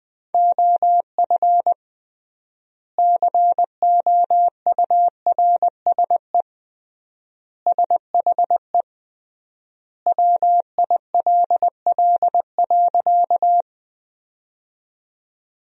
Morse code, heard as a keyed tone, about 20 wpm, 700 Hz. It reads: OF COURSE SHE WILL.
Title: 3x Word-Spacing - Sentences from Top 500 Words - 20wpm